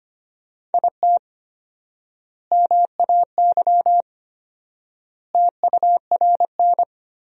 Morse code, heard as a keyed tone, about 25 wpm, 700 Hz. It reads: IT MAY TURN
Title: 4x Word-Spacing - Sentences from Top 400 Words - 25wpm